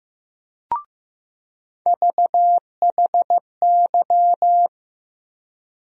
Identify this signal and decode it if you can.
Morse code, heard as a keyed tone, about 15 wpm, 700 Hz. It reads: VHY
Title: USA Call Sign Postfixes - 3 Letters 15wpm